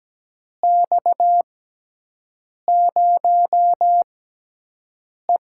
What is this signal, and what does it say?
Morse code, heard as a keyed tone, about 17 wpm, spaced as for 6 wpm, 700 Hz. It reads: X0E
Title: USA Call Signs - 6x Character Spacing - 17wpm